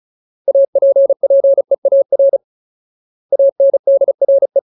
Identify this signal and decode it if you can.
Morse code, heard as a keyed tone, about 34 wpm, 550 Hz. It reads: APPEAR ANDRE
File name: Ultimate Word List / 35WPM Ultimate